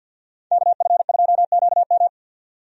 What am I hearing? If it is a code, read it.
Morse code, heard as a keyed tone, about 51 wpm, 700 Hz. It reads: KF2QM
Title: USA Call Signs - 5 Characters - 50wpm